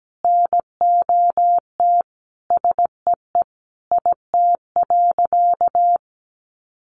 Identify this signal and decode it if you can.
Morse code, heard as a keyed tone, about 17 wpm, 700 Hz. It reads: NOT SEE IT.
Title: Sentences from Top 1000 Words 17wpm